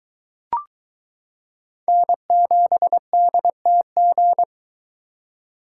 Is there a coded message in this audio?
Yes: Morse code from 1.9 to 4.4 s, heard as a keyed tone, about 23 wpm, 700 Hz. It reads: N7DTG